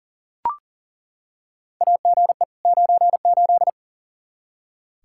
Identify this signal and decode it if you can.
Morse code, heard as a keyed tone, about 40 wpm, 700 Hz. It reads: AGE 98
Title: Practice QSO - Exchange 3 - More Info - 40wpm